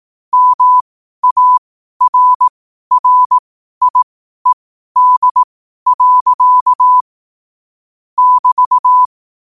Morse code, heard as a keyed tone, about 18 wpm, spaced as for 12 wpm, 1000 Hz. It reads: MARRIED. <BT>
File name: qotdfiles / 12wpm